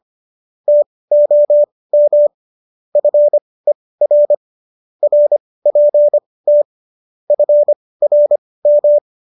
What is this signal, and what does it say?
Morse code, heard as a keyed tone, about 25 wpm, spaced as for 18 wpm, 590 Hz. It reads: TOM FER RPT FRM
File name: qso201 18